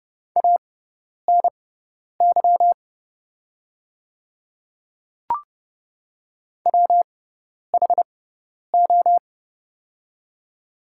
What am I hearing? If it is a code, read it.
Morse code, heard as a keyed tone, about 30 wpm, spaced as for 10 wpm, 690 Hz. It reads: ANY WHO
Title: 6x Character Spacing - Top 200 Words - 30wpm